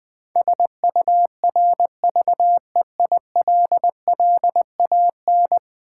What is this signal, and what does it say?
Morse code, heard as a keyed tone, about 20 wpm, 700 Hz. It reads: SURVEILLAN
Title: Build A Word - 5 Letters At A Time - 12-Letter Words - 20wpm